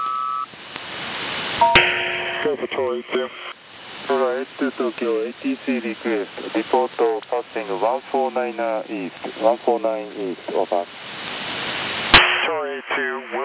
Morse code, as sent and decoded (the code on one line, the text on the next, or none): none